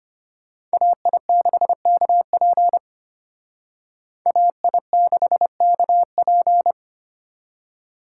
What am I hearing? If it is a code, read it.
Morse code, heard as a keyed tone, about 32 wpm, 700 Hz. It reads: AI6KP AI6KP